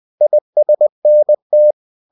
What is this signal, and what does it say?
Morse code, heard as a keyed tone, about 20 wpm, 600 Hz. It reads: ISNT
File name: cw-books / winnie-the-pooh / winnie-20wpm-page-0096